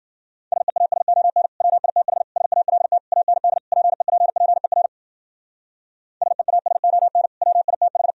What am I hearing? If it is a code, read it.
Morse code, heard as a keyed tone, about 51 wpm, 700 Hz. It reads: SEASON WITH SALT AND PEPPER SEASON WITH